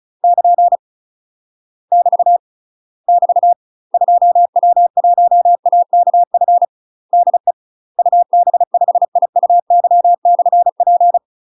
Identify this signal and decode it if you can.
Morse code, heard as a keyed tone, about 36 wpm, 700 Hz. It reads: <KN> <BT> <BT> 2W1AKF DE UB5IUY/P
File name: UK MASTER QSOs TRAINING/35WPM